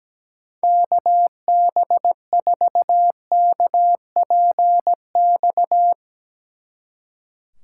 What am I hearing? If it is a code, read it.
Morse code, heard as a keyed tone, about 18 wpm, 700 Hz. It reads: KB4KPX
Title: Speed Racing - USA 2x3 Call Signs 17wpm